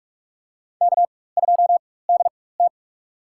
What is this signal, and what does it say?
Morse code, heard as a keyed tone, about 46 wpm, spaced as for 21 wpm, 700 Hz. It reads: K2DT
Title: USA Call Signs - 4 Characters - 4x Character Spacing - 45wpm